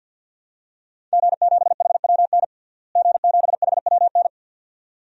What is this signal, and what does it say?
Morse code, heard as a keyed tone, about 46 wpm, 700 Hz. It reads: G7HWN G7HWN
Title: DXCC-100 Call Signs - 5 Characters - Repeated 2x - 50wpm